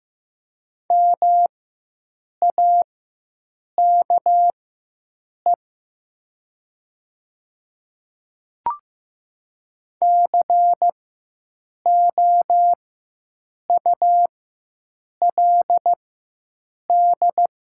Morse code, heard as a keyed tone, about 15 wpm, spaced as for 7 wpm, 700 Hz. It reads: MAKE COULD
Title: 4x Character Spacing - Top 100 Words - 15wpm